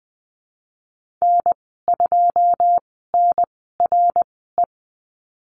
Morse code, heard as a keyed tone, about 20 wpm, spaced as for 15 wpm, 700 Hz. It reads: N2NRE
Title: 082 Lesson 14g - Number 2 - Call Signs 20wpm